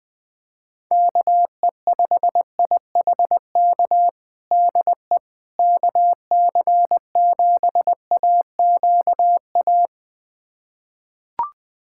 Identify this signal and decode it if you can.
Morse code, heard as a keyed tone, about 20 wpm, 700 Hz. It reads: KE5IHK DE KC7AQA
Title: USA Callsign DE Callsign - 6 Characters - 20wpm